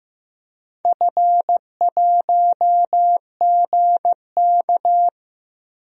Morse code, heard as a keyed tone, about 15 wpm, 700 Hz. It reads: F1GK
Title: DXCC-100 Call Signs - 4 Characters - Repeated 3x - 15wpm